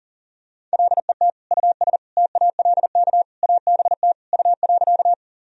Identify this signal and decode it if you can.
Morse code, heard as a keyed tone, about 40 wpm, 700 Hz. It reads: LET US TALK ABT U.
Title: Sentences from Top 400 Words and Abbreviations 40wpm